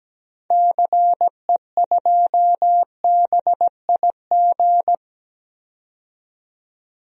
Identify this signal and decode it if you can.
Morse code, heard as a keyed tone, about 17 wpm, 700 Hz. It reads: CE2BIG
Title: DXCC-100 Call Signs - 6 Characters - 17wpm